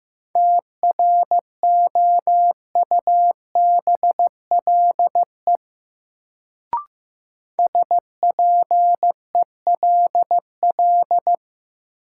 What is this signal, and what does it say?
Morse code, heard as a keyed tone, about 15 wpm, 700 Hz. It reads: TROUBLE SPELL
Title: Sentences from Top 1200 Words - New Word Drill Alt - 15wpm